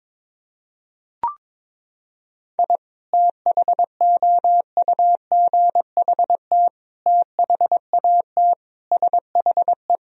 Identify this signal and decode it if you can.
Morse code, heard as a keyed tone, about 22 wpm, 700 Hz. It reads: I THOUGHT THAT SHE
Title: Sentences from Top 1000 Words - Review Part 3 - 22wpm